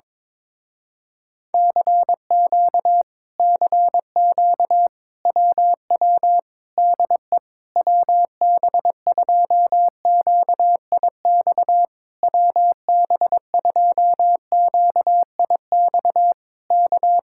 Morse code, heard as a keyed tone, about 21 wpm, 700 Hz. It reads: CQ CQ WW DE WB2QIX WB2QIX K
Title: Speed-Racing Common QSO Phrases Encore - 22wpm